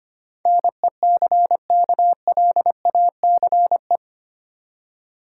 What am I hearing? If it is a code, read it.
Morse code, heard as a keyed tone, about 25 wpm, 700 Hz. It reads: NECKLACE
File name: practice / 25wpm / Easy Words with Hints - 25wpm